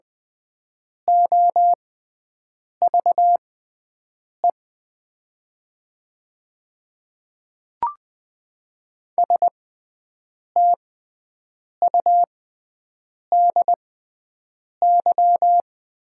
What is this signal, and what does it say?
Morse code, heard as a keyed tone, about 20 wpm, spaced as for 7 wpm, 700 Hz. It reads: OVE STUDY
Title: 6x Character Spacing - Top 500 Words - 20wpm